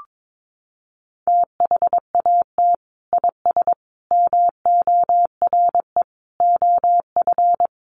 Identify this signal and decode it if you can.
Morse code, heard as a keyed tone, about 22 wpm, 700 Hz. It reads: THAT IS MORE OF
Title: Sentences from Top 1300 Words 22wpm